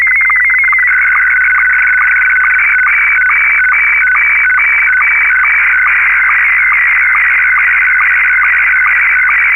SSTV (slow-scan television), Scottie 1.